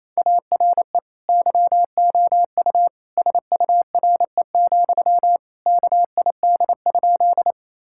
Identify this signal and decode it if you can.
Morse code, heard as a keyed tone, about 28 wpm, 700 Hz. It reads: ARE YOU SURE, KID?
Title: Sentences from Top 1600 Words 28wpm